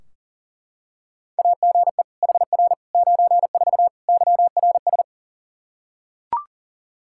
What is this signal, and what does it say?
Morse code, heard as a keyed tone, about 40 wpm, 700 Hz. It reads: AGE HR 94 YRS
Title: Practice QSO - Exchange 3 - More Info - 40wpm